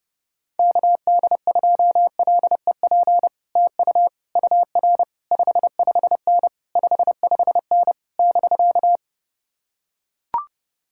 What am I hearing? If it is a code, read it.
Morse code, heard as a keyed tone, about 30 wpm, 700 Hz. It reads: KD2LEP TU UR 55N 55N <BK>